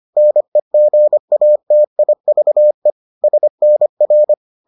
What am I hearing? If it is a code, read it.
Morse code, heard as a keyed tone, about 25 wpm, 600 Hz. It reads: NEGATIVE SNR